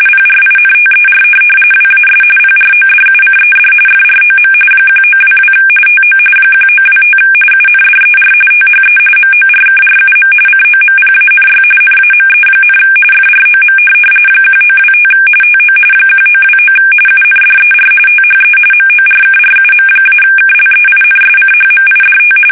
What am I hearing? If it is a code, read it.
RTTY, about 45.45 baud. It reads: PRINTER WAS PRODUCED FROM 1922 TO 1925 UNDER THE MORKRUM COMPANY NAME, FROM 1925 TO 1929 UNDER THE MORKRUM-KLEINSCHMIDT
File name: fsk850 / teletype-corp